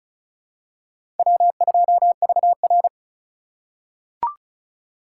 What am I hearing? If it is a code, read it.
Morse code, heard as a keyed tone, about 34 wpm, 700 Hz. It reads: W2VR
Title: USA Call Signs - 4x Character Spacing - 35wpm